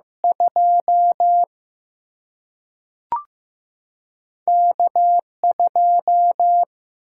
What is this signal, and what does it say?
Morse code, heard as a keyed tone, about 15 wpm, 700 Hz. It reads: 2 K 2
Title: Single-Letter Single-Number 15wpm